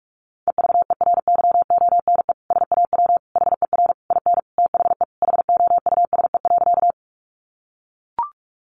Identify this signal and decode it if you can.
Morse code, heard as a keyed tone, about 44 wpm, 700 Hz. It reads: EVERYONE SAW HER IN THE HOUSE.
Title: Sentences from Top 800 Words 45wpm